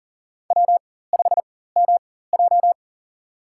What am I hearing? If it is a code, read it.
Morse code, heard as a keyed tone, about 37 wpm, spaced as for 18 wpm, 700 Hz. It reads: W5MJ
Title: USA Call Signs - 4 Characters - 4x Character Spacing - 40wpm